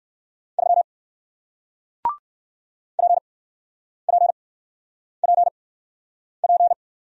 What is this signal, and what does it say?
Morse code, heard as a keyed tone, about 60 wpm, spaced as for 9 wpm, 700 Hz. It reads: 3 PPPP